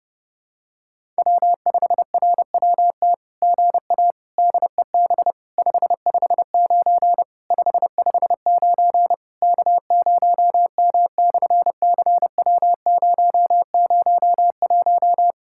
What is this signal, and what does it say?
Morse code, heard as a keyed tone, about 28 wpm, 700 Hz. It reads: W5RWT GA DEB 559 559 K0M/CW001
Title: SOTA QSO Practice 30wpm